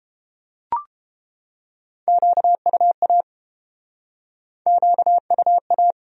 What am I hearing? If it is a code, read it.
Morse code, heard as a keyed tone, about 30 wpm, 700 Hz. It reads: QUA QUA